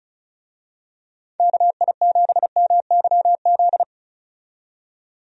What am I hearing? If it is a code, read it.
Morse code, heard as a keyed tone, about 36 wpm, 700 Hz. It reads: KI7MYZ
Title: USA Call Signs - 6 Characters - 35wpm